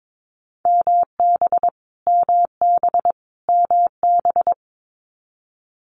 Morse code, heard as a keyed tone, about 22 wpm, 700 Hz. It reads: MB MB MB
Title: Canadian Provinces and Territories - Repeated 3x - 22wpm